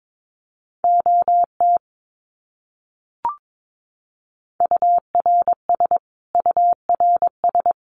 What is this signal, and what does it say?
Morse code, heard as a keyed tone, about 22 wpm, 700 Hz. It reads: OT URS URS
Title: Abbreviations Repeated 3x - 22wpm